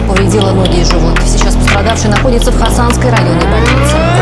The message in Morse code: none